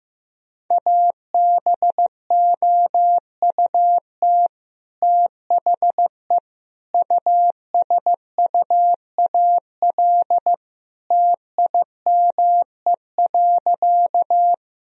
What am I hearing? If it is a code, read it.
Morse code, heard as a keyed tone, about 15 wpm, 700 Hz. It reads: ABOUT THE USUAL TIME.
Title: Sentences from Top 1000 Words - Review Part 2 - 15wpm